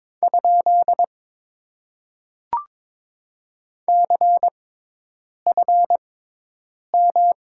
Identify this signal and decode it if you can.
Morse code, heard as a keyed tone, about 21 wpm, spaced as for 8 wpm, 700 Hz. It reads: ? CFM
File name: Top 90 QSO Elements - 6x Character Spacing - 22wpm